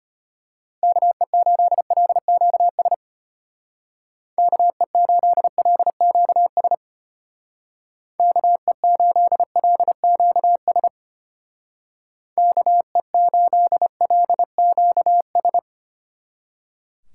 Morse code, heard as a keyed tone, about 30 wpm, 700 Hz. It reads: KE8LQS KE8LQS KE8LQS KE8LQS
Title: Speed Racing - USA 2x3 Call Signs 25wpm